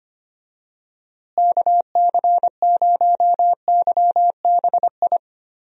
Morse code, heard as a keyed tone, about 25 wpm, 700 Hz. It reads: KC0YBI
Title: USA Call Signs - 6 Characters - Repeated 2x - 25wpm